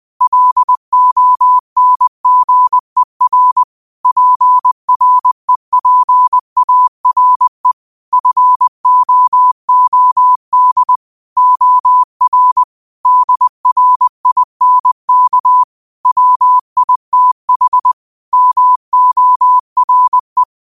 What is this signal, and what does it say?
Morse code, heard as a keyed tone, about 20 wpm, 1000 Hz. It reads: LONGER PREPARE FOOD OR DRINK WITH MORE
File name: 20wpm